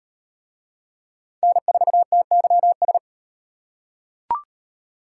Morse code, heard as a keyed tone, about 34 wpm, 700 Hz. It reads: N4TYS